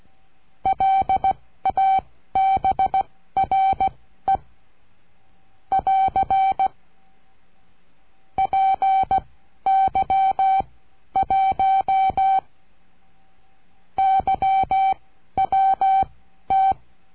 Morse code, heard as a keyed tone, about 16 wpm, spaced as for 10 wpm, 770 Hz. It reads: LABRE <AR> PY1YWT
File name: CURSO DE TELEGRAFIA